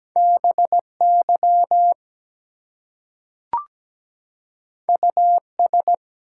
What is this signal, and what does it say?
Morse code, heard as a keyed tone, about 17 wpm, 700 Hz. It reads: BY US
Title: Two Letter Words - 17wpm